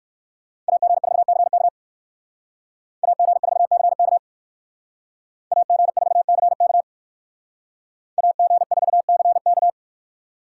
Morse code, heard as a keyed tone, about 64 wpm, 700 Hz. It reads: AG4CK AG4CK AG4CK AG4CK